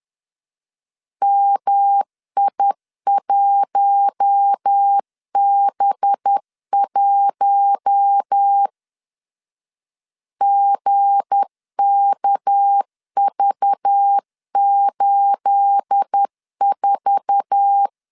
Morse code, heard as a keyed tone, about 10 wpm, 800 Hz. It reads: MI1B1 GKV84
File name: CW / DET / 1 9 1500C S A 10wpm 4511